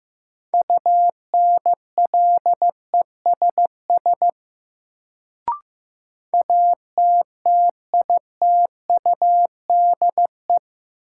Morse code, heard as a keyed tone, about 15 wpm, 700 Hz. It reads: UNLESS ATTITUDE
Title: Sentences from Top 1300 Words - New Word Drill Alt - 15wpm